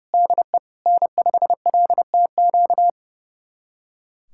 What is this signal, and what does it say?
Morse code, heard as a keyed tone, about 30 wpm, 700 Hz. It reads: DE N5LTQ
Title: USA Callsign DE Callsign - 5 Characters - 30wpm